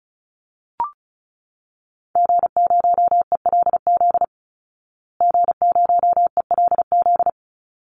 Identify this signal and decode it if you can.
Morse code, heard as a keyed tone, about 34 wpm, 700 Hz. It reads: G0ELZ G0ELZ
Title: Call Signs Repeated 3x - 35wpm